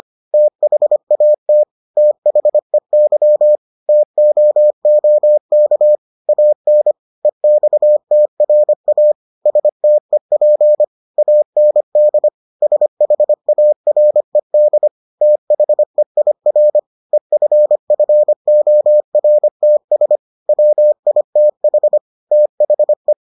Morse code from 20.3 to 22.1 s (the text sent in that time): WITH